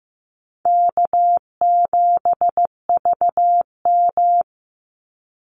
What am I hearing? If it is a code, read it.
Morse code, heard as a keyed tone, about 15 wpm, 700 Hz. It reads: K7VM
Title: Easy Call Signs Repeated 2x - 15wpm